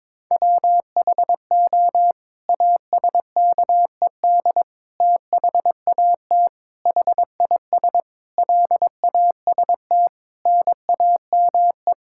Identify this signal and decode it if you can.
Morse code, heard as a keyed tone, about 22 wpm, 700 Hz. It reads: WHO ASKED THAT HIS LAST NAME
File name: Sets of 9 Words - 22wpm